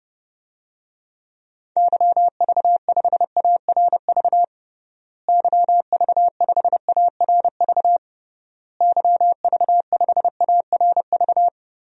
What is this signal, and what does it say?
Morse code, heard as a keyed tone, about 32 wpm, 700 Hz. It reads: YV5ARV YV5ARV YV5ARV
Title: DXCC-100 Call Signs - 6 Characters - Repeated 3x - 30wpm